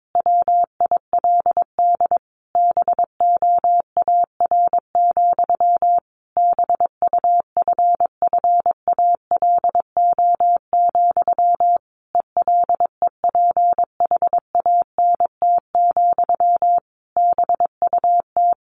Morse code, heard as a keyed tone, about 22 wpm, 700 Hz. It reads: WILD BOAR, BUFFALO, ELEPHANT, BUT
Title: A Sound Of Thunder 22wpm